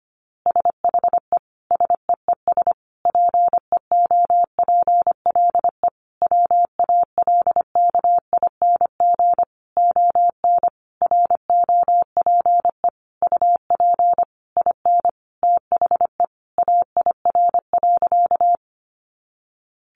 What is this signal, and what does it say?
Morse code, heard as a keyed tone, about 25 wpm, 700 Hz. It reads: SHE SEES PEOPLE WALKING ON ROPE UP IN THE AIR.
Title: Easy Phrases 25wpm